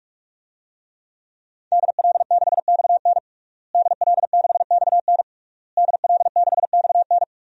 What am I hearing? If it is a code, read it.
Morse code, heard as a keyed tone, about 46 wpm, 700 Hz. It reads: DL6XN DL6XN DL6XN
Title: DXCC-100 Call Signs - 5 Characters - Repeated 3x - 45wpm